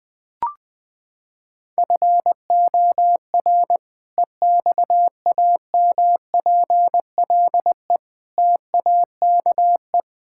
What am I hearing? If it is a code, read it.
Morse code, heard as a keyed tone, about 20 wpm, 700 Hz. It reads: FOR EXAMPLE TAKE
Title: Sentences with 2-Word Phrases - 20wpm